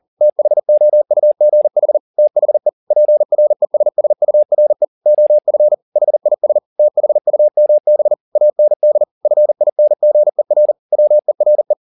Morse code, heard as a keyed tone, about 40 wpm, 600 Hz. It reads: THOUGH THE PRESSURE OF HIS THUMB AND FINGER WERE